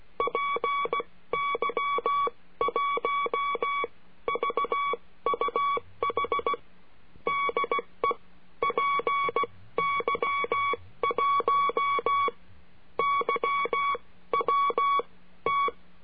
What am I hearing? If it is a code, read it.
Morse code, heard as a keyed tone, about 16 wpm, spaced as for 13 wpm, 1130 Hz. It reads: PY1VUH DE PY1YWT
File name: CURSO DE TELEGRAFIA